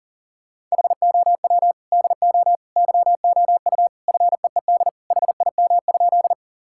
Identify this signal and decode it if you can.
Morse code, heard as a keyed tone, about 40 wpm, 700 Hz. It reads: HOW DO YOU FEED HIM?
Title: Sentences from Top 1600 Words 40wpm